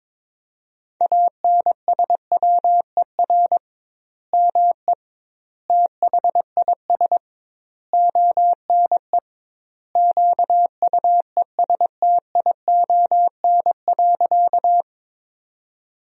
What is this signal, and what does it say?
Morse code, heard as a keyed tone, about 22 wpm, 700 Hz. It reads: ANSWER ME THIS ONE QUESTION.
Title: 2x Word-Spacing - Sentences from Top 400 Words - 22wpm